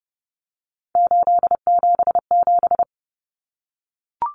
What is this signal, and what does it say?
Morse code, heard as a keyed tone, about 30 wpm, 700 Hz. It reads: 877
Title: Common Cut Numbers - 3-Digit 30wpm